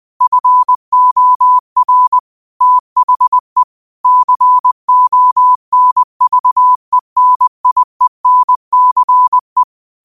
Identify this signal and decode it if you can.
Morse code, heard as a keyed tone, about 20 wpm, 1000 Hz. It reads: FOR THE CONVENIENCE
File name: qotdfiles / 20wpm